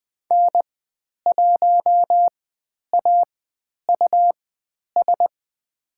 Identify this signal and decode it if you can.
Morse code, heard as a keyed tone, about 20 wpm, spaced as for 10 wpm, 700 Hz. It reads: N1AUS
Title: 109 Lesson 18g - Letter m - Call Signs 20wpm